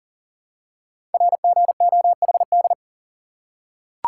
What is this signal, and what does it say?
Morse code, heard as a keyed tone, about 42 wpm, 700 Hz. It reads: RGOHD
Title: Code Groups - 5 Letters - 40wpm